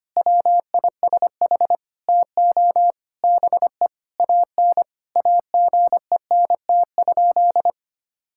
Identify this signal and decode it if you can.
Morse code, heard as a keyed tone, about 25 wpm, 700 Hz. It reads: WISH TO BE AN AGENT?
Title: Sentences from Top 1600 Words 25wpm